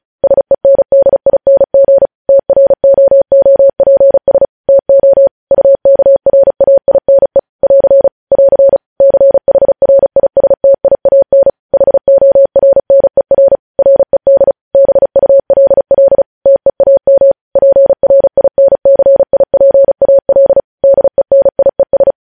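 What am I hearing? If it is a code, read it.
Morse code, heard as a keyed tone, about 36 wpm, 550 Hz. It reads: SENDING TROOPS TO UKRAINE <AR> <AR> CHRISTIAN HORNER RED BULL TEAM PRINCIPAL DENIES
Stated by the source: News headlines sent in Morse at speed 35.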